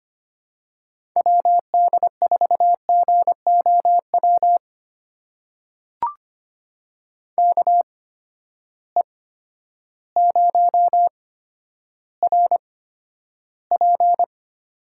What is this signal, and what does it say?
Morse code, heard as a keyed tone, about 25 wpm, spaced as for 11 wpm, 700 Hz. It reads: WD4GOW KE0RP
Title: USA Call Signs - 6 Characters - 8x Character Spacing - 25wpm